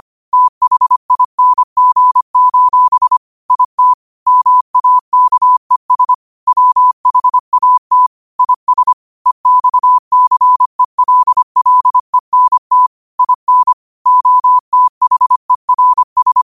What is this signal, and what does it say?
Morse code, heard as a keyed tone, about 25 wpm, 1000 Hz. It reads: THING: IT MAKES WHAT IS EXCELLENT IN OTHERS
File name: qotdfiles / 25wpm